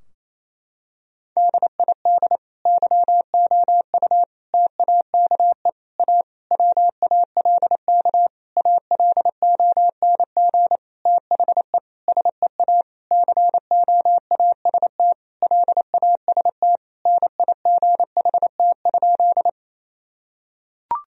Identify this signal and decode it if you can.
Morse code, heard as a keyed tone, about 28 wpm, 700 Hz. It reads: DID YOU TAKE A WALK ALONG THE SEA COAST LAST NIGHT?
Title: Sentences from Top 1000 Words - Review Part 4 - 28wpm